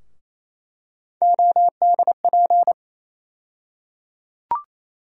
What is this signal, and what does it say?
Morse code, heard as a keyed tone, about 28 wpm, 700 Hz. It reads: ODP
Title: USA Call Sign Postfixes - 3 Letters 28wpm